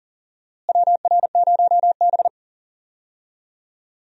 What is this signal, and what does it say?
Morse code, heard as a keyed tone, about 39 wpm, 700 Hz. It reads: WR0B
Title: USA Call Signs - 4 Characters - Repeated 2x - 40wpm